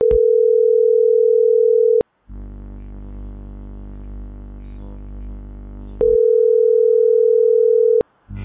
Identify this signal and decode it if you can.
Ringback tone (North America), 440 + 480 Hz.